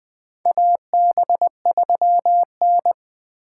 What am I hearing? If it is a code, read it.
Morse code, heard as a keyed tone, about 20 wpm, 700 Hz. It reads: AB3N
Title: USA Call Signs - 6x Character Spacing - 20wpm